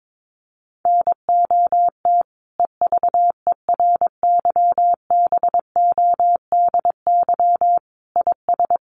Morse code, heard as a keyed tone, about 22 wpm, 700 Hz. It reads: NOT EVERYBODY IS
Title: Sentences from Top 1400 Words 22wpm